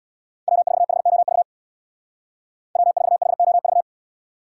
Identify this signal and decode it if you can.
Morse code, heard as a keyed tone, about 61 wpm, 700 Hz. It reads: W4HOV W4HOV
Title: Speed Racing - USA 1x3 Call Signs 50wpm